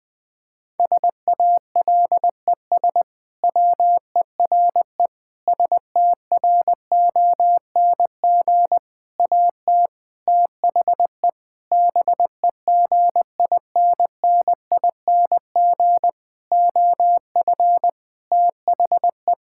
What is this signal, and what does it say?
Morse code, heard as a keyed tone, about 21 wpm, 700 Hz. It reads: SALES WERE STRONG AT THE BEGINNING OF THE
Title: Sentences with 6-Word Phrases - 20wpm